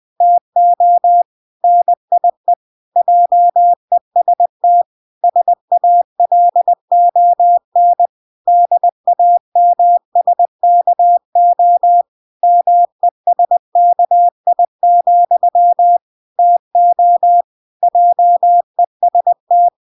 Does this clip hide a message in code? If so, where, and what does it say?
Yes: Morse code from 0.2 to 19.7 s, heard as a keyed tone, about 20 wpm, 700 Hz. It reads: TO NIE JEST SALON DAMSKO MESKI, TO JEST